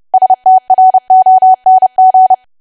Morse code, heard as a keyed tone, about 30 wpm, 750 Hz. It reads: STRONG
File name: cwa / W1AW-Level3 / 30WPM